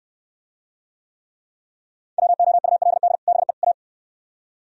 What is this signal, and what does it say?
Morse code, heard as a keyed tone, about 63 wpm, 700 Hz. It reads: WOULD BE A